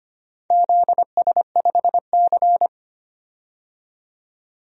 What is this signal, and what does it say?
Morse code, heard as a keyed tone, about 25 wpm, 690 Hz. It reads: ZS5C
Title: DXCC-100 Call Signs - 4 Characters - 25wpm